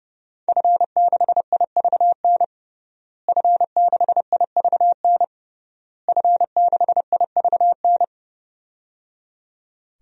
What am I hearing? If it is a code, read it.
Morse code, heard as a keyed tone, about 30 wpm, 700 Hz. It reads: F6IVN F6IVN F6IVN
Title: DXCC-100 Call Signs - 5 Characters - Repeated 3x - 30wpm